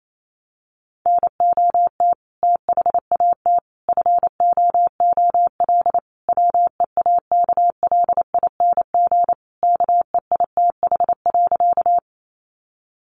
Morse code, heard as a keyed tone, about 28 wpm, 700 Hz. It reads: NOT THAT FOOL WEAKLING KEITH.
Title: A Sound Of Thunder 28wpm